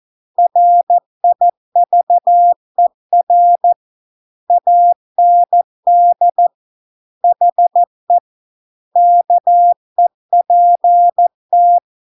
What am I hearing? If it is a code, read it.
Morse code, heard as a keyed tone, about 14 wpm, 700 Hz. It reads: RIVER AND HE KEPT